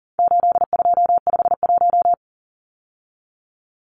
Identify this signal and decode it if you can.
Morse code, heard as a keyed tone, about 40 wpm, 700 Hz. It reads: 8251
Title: Numbers - 4-Digit 40wpm